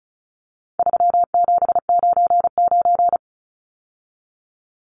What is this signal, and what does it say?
Morse code, heard as a keyed tone, about 34 wpm, 700 Hz. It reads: 3799
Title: Numbers - 4-Digit 35wpm